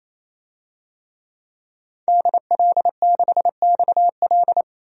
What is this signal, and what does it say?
Morse code, heard as a keyed tone, about 28 wpm, 700 Hz. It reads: DL6XL
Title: DXCC-100 Call Signs - 5 Characters - 28wpm